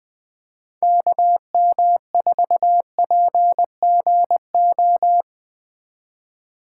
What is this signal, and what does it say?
Morse code, heard as a keyed tone, about 19 wpm, 700 Hz. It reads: KM4PGO